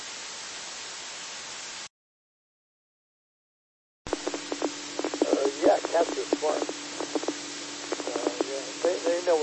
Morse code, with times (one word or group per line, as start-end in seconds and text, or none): none